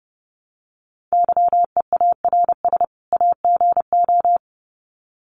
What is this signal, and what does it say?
Morse code, heard as a keyed tone, about 30 wpm, 700 Hz. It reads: YEARS AGO
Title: Sets of 2 Words 30wpm